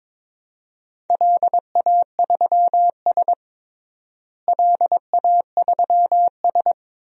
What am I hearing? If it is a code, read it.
Morse code, heard as a keyed tone, about 22 wpm, 700 Hz. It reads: LA3S LA3S
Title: DXCC-100 Call Signs - 4 Characters - Repeated 2x - 22wpm